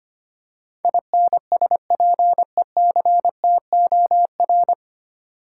Morse code, heard as a keyed tone, about 25 wpm, 700 Hz. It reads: INSPECTOR
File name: practice / 25wpm / 9 Letter Words - First 3 Letters Given - 25wpm